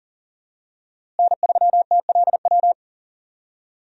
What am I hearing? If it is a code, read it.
Morse code, heard as a keyed tone, about 40 wpm, 700 Hz. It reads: N3TLW
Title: USA Call Signs - 5 Characters - Repeated 3x - 40wpm